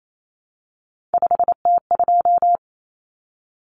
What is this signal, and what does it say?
Morse code, heard as a keyed tone, about 28 wpm, 700 Hz. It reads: 5T2
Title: Common Cut Numbers - 3-Digit 28wpm